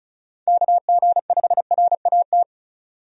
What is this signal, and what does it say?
Morse code, heard as a keyed tone, about 36 wpm, 700 Hz. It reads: KG5RAT
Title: USA Call Signs - 6 Characters - 35wpm